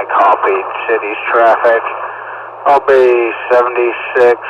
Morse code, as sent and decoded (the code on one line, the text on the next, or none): none